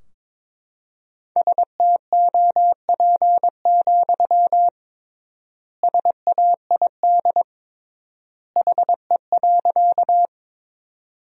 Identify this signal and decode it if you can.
Morse code, heard as a keyed tone, about 21 wpm, 700 Hz. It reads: STOP, SAID HE.
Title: 3x Word-Spacing - Sentences from Top 300 Words - 22wpm